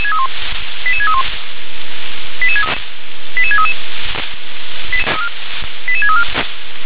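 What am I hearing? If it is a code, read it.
Five-tone selective call (ZVEI), heard as quick runs of five tones, tones of about 70 ms. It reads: 88631 88633 88634
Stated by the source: ZVEI five-tone selective call 88712 received with an SDR.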